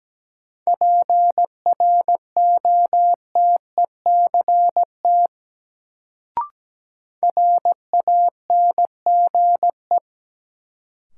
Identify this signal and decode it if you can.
Morse code, heard as a keyed tone, about 17 wpm, 700 Hz. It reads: PROTECT RANGE
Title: Sentences from Top 900 Words - New Word Drill - 17wpm